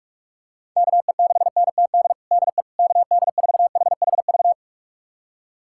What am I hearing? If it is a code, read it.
Morse code, heard as a keyed tone, about 42 wpm, 700 Hz. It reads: KE6NTD DE KD4HHV
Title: USA Callsign DE Callsign 45wpm